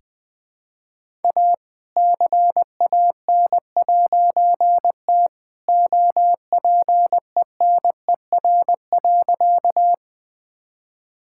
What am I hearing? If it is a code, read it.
Morse code, heard as a keyed tone, about 20 wpm, 700 Hz. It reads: A CAN'T OPENER.
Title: All-Age Appropriate Jokes - 20wpm